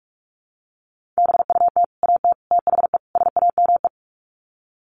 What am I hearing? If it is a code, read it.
Morse code, heard as a keyed tone, about 44 wpm, 700 Hz. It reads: BUT AT THE SAME
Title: Sets of 4 Words 45wpm